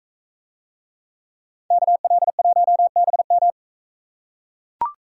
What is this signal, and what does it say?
Morse code, heard as a keyed tone, about 42 wpm, 700 Hz. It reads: KL1BM